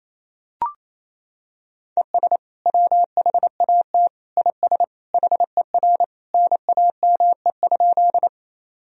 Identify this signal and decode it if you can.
Morse code, heard as a keyed tone, about 28 wpm, 700 Hz. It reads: ES WHAT IS HER NAME?
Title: Sentences from Top 200 Words and Abbreviations 28wpm